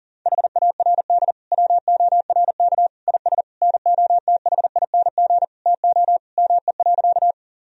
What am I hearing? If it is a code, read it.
Morse code, heard as a keyed tone, about 37 wpm, 700 Hz. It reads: HARD WORK IS NOTHING TO ME.
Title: Sentences from Top 1000 Words - Review Part 1 - 40wpm